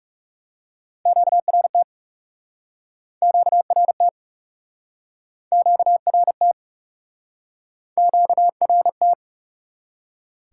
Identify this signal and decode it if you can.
Morse code, heard as a keyed tone, about 42 wpm, 700 Hz. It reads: QRT QRT QRT QRT